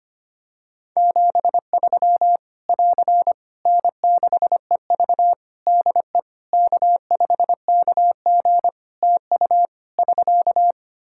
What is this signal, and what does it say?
Morse code, heard as a keyed tone, about 25 wpm, 700 Hz. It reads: 73 <AR> N6EV DE K5KG TU <SK>
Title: K1USN SST QSO Practice 25wpm